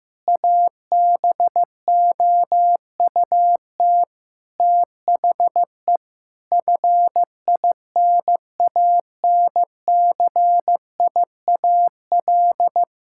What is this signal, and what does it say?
Morse code, heard as a keyed tone, about 15 wpm, 700 Hz. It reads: ABOUT THE FINANCIAL
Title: Sentences from Top 1000 Words - Review Part 5 - 15wpm